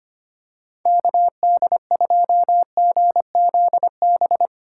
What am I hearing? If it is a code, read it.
Morse code, heard as a keyed tone, about 25 wpm, 700 Hz. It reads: KD2GZB
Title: USA Call Signs - 6 Characters - 4x Character Spacing - 25wpm